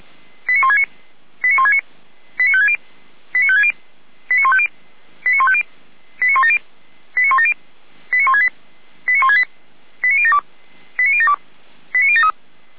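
Five-tone selective call (ZVEI), heard as quick runs of five tones, tones of about 70 ms. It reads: 89169 89169 89560 89560 89150 89150 89170 89170 89168 89168 89072 89072 89073
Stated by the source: ZVEI five-tone call 89223 received on SDR, Feuerwehr Orlamünde server.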